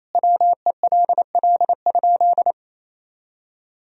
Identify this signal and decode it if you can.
Morse code, heard as a keyed tone, about 28 wpm, 700 Hz. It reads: WELL?
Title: Sentences from Top 400 Words and Abbreviations 28wpm